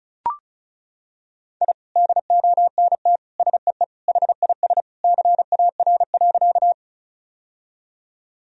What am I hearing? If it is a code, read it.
Morse code, heard as a keyed tone, about 34 wpm, 700 Hz. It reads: I DONT SEE HIS CAR.
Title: Sentences from Top 1000 Words - Review Part 1 - 35wpm